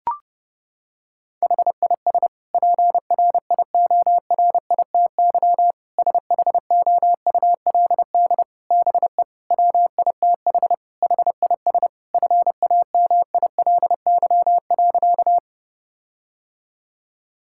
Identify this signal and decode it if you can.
Morse code, heard as a keyed tone, about 30 wpm, 700 Hz. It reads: HIS PRIORITY SHOULD BE WITH HIS FAMILY.
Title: Sentences from Top 1600 Words 30wpm